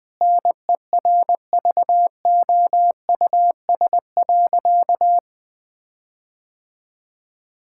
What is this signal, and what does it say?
Morse code, heard as a keyed tone, about 20 wpm, 700 Hz. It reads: NERVOUS.
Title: Sentences with 5-Word Phrases - 20wpm